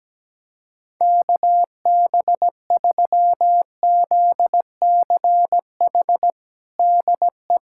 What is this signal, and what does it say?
Morse code, heard as a keyed tone, about 17 wpm, 700 Hz. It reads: KB3ZCH DE
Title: USA Callsign DE Callsign 17wpm